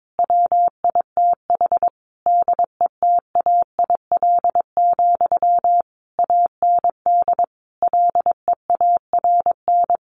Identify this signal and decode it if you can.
Morse code, heard as a keyed tone, about 22 wpm, 700 Hz. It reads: WITH DETAIL, AND LEARN
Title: Sentences from Top 1500 Words 22wpm